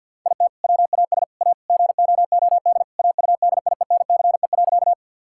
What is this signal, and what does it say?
Morse code, heard as a keyed tone, about 50 wpm, 700 Hz. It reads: IT WAS A GOOD AUDIENCE.
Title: Sentences from Top 1000 Words - Review Part 5 - 50wpm